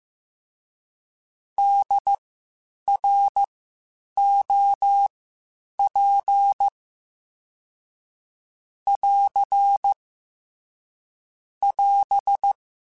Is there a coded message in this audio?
Yes: Morse code from 1.6 to 12.5 s, heard as a keyed tone, about 15 wpm, spaced as for 8 wpm, 800 Hz. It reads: DROP <AR> <AS>